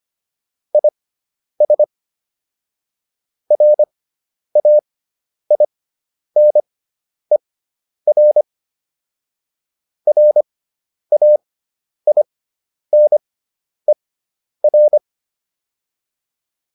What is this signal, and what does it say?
Morse code, heard as a keyed tone, about 25 wpm, spaced as for 10 wpm, 600 Hz. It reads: IS RAINER RAINER
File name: QSO405 10